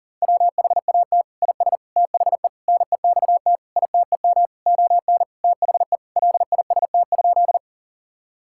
Morse code, heard as a keyed tone, about 40 wpm, 700 Hz. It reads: WHAT IS THE NEXT ITEM ON THE LIST?
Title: Sentences from Top 1000 Words - Review Part 6 - 40wpm